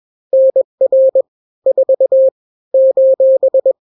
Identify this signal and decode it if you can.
Morse code, heard as a keyed tone, about 21 wpm, 530 Hz. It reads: NR 4 :
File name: daily / 20wpm